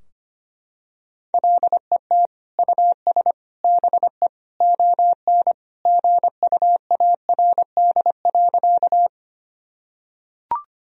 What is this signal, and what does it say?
Morse code, heard as a keyed tone, about 25 wpm, 700 Hz. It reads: LET US BE ON GUARD.
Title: Sentences from Top 1000 Words - Review Part 4 - 25wpm